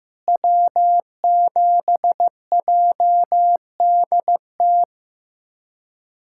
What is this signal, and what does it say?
Morse code, heard as a keyed tone, about 15 wpm, 700 Hz. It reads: W7JDT
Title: USA Callsign DE Callsign - 5 Characters - 15wpm